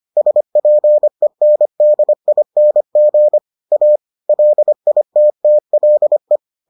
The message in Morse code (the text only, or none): SPENDING A LITTLE